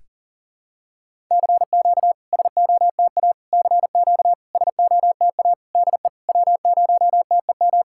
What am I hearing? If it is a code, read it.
Morse code, heard as a keyed tone, about 41 wpm, 690 Hz. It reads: CQ SOTA CQ SOTA DE W0TEM
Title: SOTA QSO Practice 40wpm